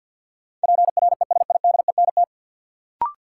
Morse code, heard as a keyed tone, about 52 wpm, 700 Hz. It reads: PRESIDENT
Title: Sets of 2 Words - New Word Drill - 50wpm